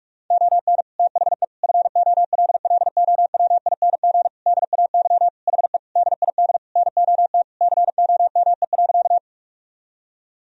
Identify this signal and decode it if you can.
Morse code, heard as a keyed tone, about 46 wpm, 700 Hz. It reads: ON THE FOLLOWING DAY HE DID NOT COME.
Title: Sentences from Top 1000 Words - Review Part 4 - 45wpm